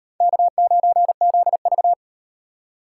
Morse code, heard as a keyed tone, about 38 wpm, 700 Hz. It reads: K9ZV